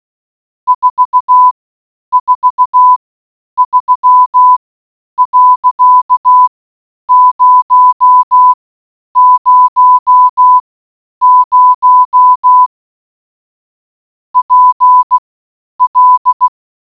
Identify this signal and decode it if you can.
Morse code, heard as a keyed tone, about 16 wpm, spaced as for 9 wpm, 1000 Hz. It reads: 443.000 PL